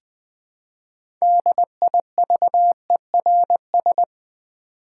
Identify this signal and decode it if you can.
Morse code, heard as a keyed tone, about 20 wpm, 700 Hz. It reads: DIVERS
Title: Build A Word - 3 Letters At A Time - 9-Letter Words - 20wpm